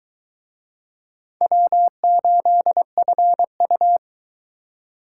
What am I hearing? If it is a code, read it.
Morse code, heard as a keyed tone, about 22 wpm, 700 Hz. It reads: W8FU